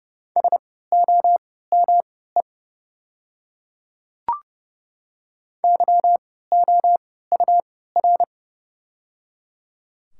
Morse code, heard as a keyed tone, about 30 wpm, spaced as for 17 wpm, 700 Hz. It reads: SOME YOUR
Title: 3x Character Spacing - Top 300 Words - 30wpm